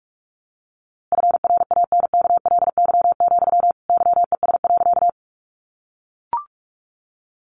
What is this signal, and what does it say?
Morse code, heard as a keyed tone, about 47 wpm, 700 Hz. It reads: FRANKLY, YES.
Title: A Sound Of Thunder 45wpm